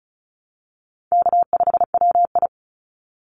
Morse code, heard as a keyed tone, about 34 wpm, 700 Hz. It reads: K5WI
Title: NA SOTA Chaser Callsigns 35wpm